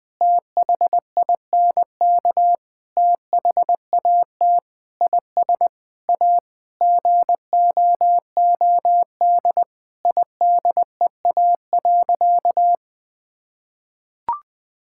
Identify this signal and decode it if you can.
Morse code, heard as a keyed tone, about 20 wpm, 700 Hz. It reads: THINK THAT IS A GOOD IDEA.
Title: Sentences with 5-Word Phrases - 20wpm